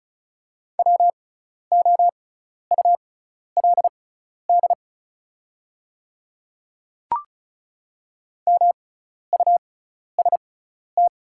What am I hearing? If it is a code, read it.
Morse code, heard as a keyed tone, about 36 wpm, spaced as for 12 wpm, 690 Hz. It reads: WOULD MUST
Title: 6x Character Spacing - Top 300 Words - 35wpm